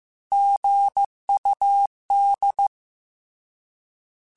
Morse code, heard as a keyed tone, about 15 wpm, 800 Hz. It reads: GUD
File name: Abreviations / Abreviations mp3s / Abreviations@13